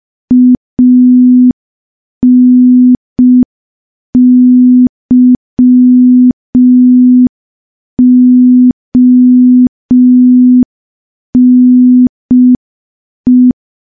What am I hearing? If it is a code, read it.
Morse code, heard as a keyed tone, about 5 wpm, 250 Hz. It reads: ANYONE
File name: morse / morse-training-05WPM